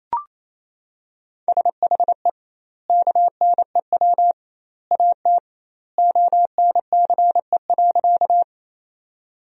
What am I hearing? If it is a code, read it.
Morse code, heard as a keyed tone, about 28 wpm, 700 Hz. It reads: SHE KNEW AT ONCE.
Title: 2x Word-Spacing - Sentences from Top 400 Words - 28wpm